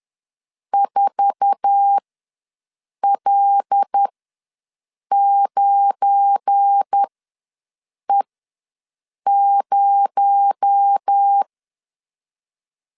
Morse code, heard as a keyed tone, about 10 wpm, spaced as for 6 wpm, 800 Hz. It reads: 4L9E0
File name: LEZIONI / 10 SIGNS / 10A SLASH ws 10wpm 4121